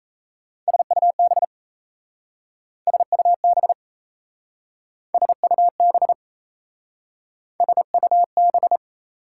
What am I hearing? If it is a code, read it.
Morse code, heard as a keyed tone, about 36 wpm, 690 Hz. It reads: SUB SUB SUB SUB